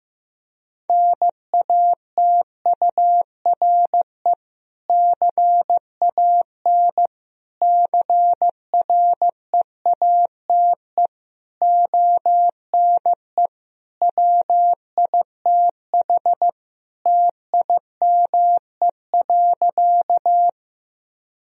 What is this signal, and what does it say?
Morse code, heard as a keyed tone, about 15 wpm, 700 Hz. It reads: NATURE CAN CREATE ONE WITH TIME.
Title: Sentences from Top 1000 Words - Review Part 3 - 15wpm